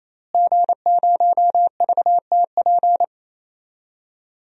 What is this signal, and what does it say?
Morse code, heard as a keyed tone, about 27 wpm, 700 Hz. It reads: G0VTP
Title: DXCC-100 Call Signs - 5 Characters - Repeated 3x - 28wpm